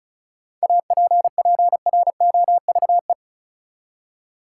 Morse code, heard as a keyed tone, about 36 wpm, 700 Hz. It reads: APPROVE
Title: Seven Letter Words - 35wpm